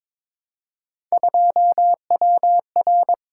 Morse code, heard as a keyed tone, about 22 wpm, 700 Hz. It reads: 2WR
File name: practice / 22wpm / Three Letter-Number Combinations - 22wpm